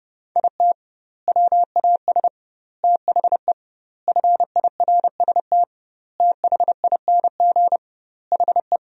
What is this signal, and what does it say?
Morse code, heard as a keyed tone, about 30 wpm, 700 Hz. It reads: IT WAS THE FIRST THING HE
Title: 2x Word-Spacing - Sentences from Top 300 Words - 30wpm